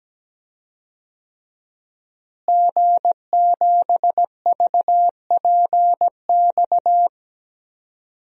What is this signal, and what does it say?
Morse code, heard as a keyed tone, about 17 wpm, 700 Hz. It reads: G7VPX
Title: DXCC-100 Call Signs - 5 Characters - Repeated 3x - 17wpm